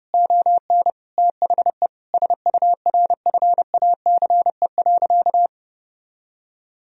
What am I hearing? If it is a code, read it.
Morse code, heard as a keyed tone, about 30 wpm, 700 Hz. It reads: ON THE SURFACE.
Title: Sentences from Top 1000 Words - Review Part 3 - 30wpm